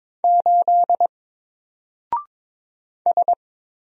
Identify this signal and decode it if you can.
Morse code, heard as a keyed tone, about 22 wpm, 700 Hz. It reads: 8 S
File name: Single-Letter-Number-22wpm